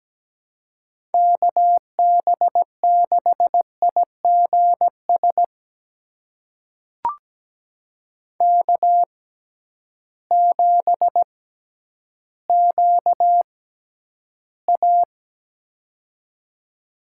Morse code, heard as a keyed tone, about 17 wpm, spaced as for 10 wpm, 700 Hz. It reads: KB6IGS K7QA
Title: USA Call Signs - 6x Character Spacing - 17wpm